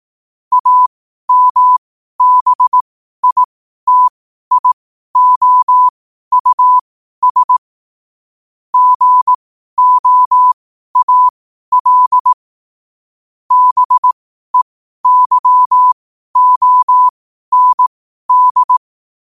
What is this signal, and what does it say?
Morse code, heard as a keyed tone, about 18 wpm, spaced as for 12 wpm, 1000 Hz. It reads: AMBITIOUS GOAL BEYOND
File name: qotdfiles / 12wpm